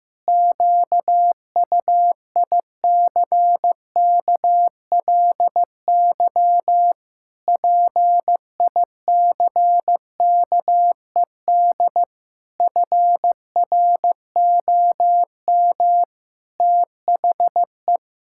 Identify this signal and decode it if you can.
Morse code, heard as a keyed tone, about 15 wpm, 700 Hz. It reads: QUICKLY PICKED FROM THE
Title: Pangram Challenge v2 - 15wpm